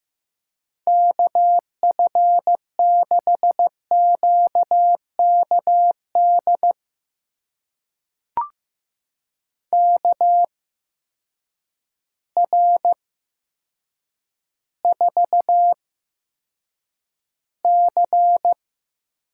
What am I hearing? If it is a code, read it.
Morse code, heard as a keyed tone, about 15 wpm, spaced as for 8 wpm, 700 Hz. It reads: KF6QKD KR4C
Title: USA Call Signs - 6 Characters - 8x Character Spacing - 15wpm